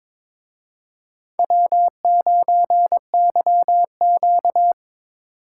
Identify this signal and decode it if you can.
Morse code, heard as a keyed tone, about 26 wpm, 700 Hz. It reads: W9YQ
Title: Speed Racing - USA 1x2 Call Signs 22wpm